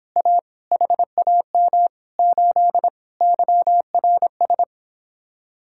Morse code, heard as a keyed tone, about 26 wpm, 700 Hz. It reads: A HAM 8 YRS